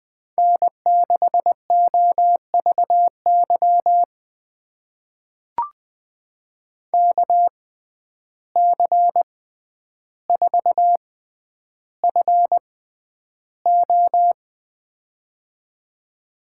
Morse code, heard as a keyed tone, about 20 wpm, spaced as for 10 wpm, 700 Hz. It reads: N6OVY KC4FO
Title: USA Call Signs - 6x Character Spacing - 20wpm